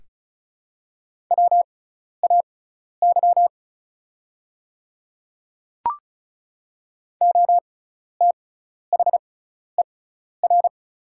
Morse code, heard as a keyed tone, about 36 wpm, spaced as for 12 wpm, 700 Hz. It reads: WAY OTHER
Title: 6x Character Spacing - Top 100 Words - 35wpm